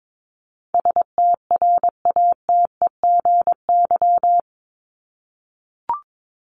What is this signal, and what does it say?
Morse code, heard as a keyed tone, about 22 wpm, 700 Hz. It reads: STRATEGY
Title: Top 501-750 Words 22wpm